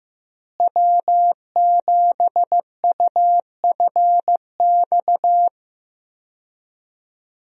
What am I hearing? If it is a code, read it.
Morse code, heard as a keyed tone, about 15 wpm, 700 Hz. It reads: W7UFX
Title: USA Call Signs - 5 Characters - Repeated 2x - 15wpm